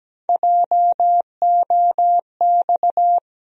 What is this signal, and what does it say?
Morse code, heard as a keyed tone, about 17 wpm, 700 Hz. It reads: JOX
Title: USA Call Sign Postfixes - 3 Letters 17wpm